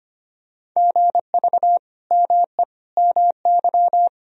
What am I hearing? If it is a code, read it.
Morse code, heard as a keyed tone, about 25 wpm, 700 Hz. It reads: GV ME MY
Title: Sentences from Top 200 Words and Abbreviations 25wpm